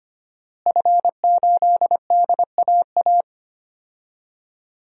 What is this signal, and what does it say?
Morse code, heard as a keyed tone, about 25 wpm, 700 Hz. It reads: F8DAA
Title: DXCC-100 Call Signs - 5 Characters - Repeated 3x - 25wpm